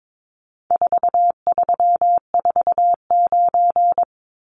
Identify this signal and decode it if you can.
Morse code, heard as a keyed tone, about 22 wpm, 700 Hz. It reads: 4349
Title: Numbers - 4-Digit 22wpm